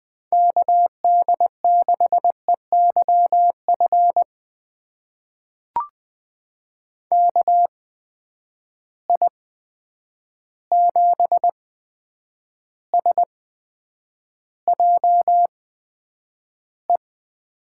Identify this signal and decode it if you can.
Morse code, heard as a keyed tone, about 20 wpm, spaced as for 9 wpm, 700 Hz. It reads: KD6EYF KI7SJE
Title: USA Call Signs - 6 Characters - 8x Character Spacing - 20wpm